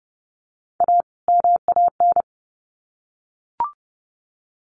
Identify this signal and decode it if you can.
Morse code, heard as a keyed tone, about 30 wpm, 700 Hz. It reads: A MAN
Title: Sets of 2 Words 30wpm